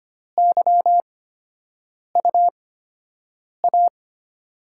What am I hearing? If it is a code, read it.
Morse code, heard as a keyed tone, about 25 wpm, spaced as for 7 wpm, 700 Hz. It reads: YUA
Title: USA Call Signs - 5 Characters - 8x Character Spacing - 25wpm